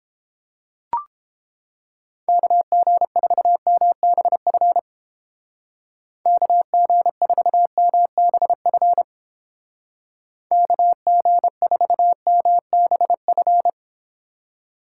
Morse code, heard as a keyed tone, about 29 wpm, 700 Hz. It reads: KG4MBF KG4MBF KG4MBF